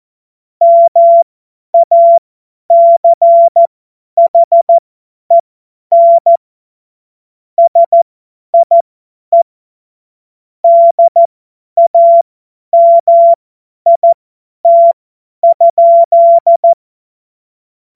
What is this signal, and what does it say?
Morse code, heard as a keyed tone, about 14 wpm, spaced as for 10 wpm, 680 Hz. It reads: MACHEN SIE DAMIT?
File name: Morsekurs / Witze / Witz 14. Juni 2022 50 F70